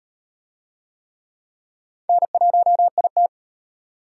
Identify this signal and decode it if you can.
Morse code, heard as a keyed tone, about 37 wpm, 700 Hz. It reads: N1IT